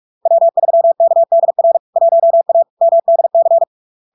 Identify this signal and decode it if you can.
Morse code, heard as a keyed tone, about 46 wpm, 670 Hz. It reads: W3KDR 1A MDC
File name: LARK-FD-2022-Log-in-CW-45wpm-670-Hz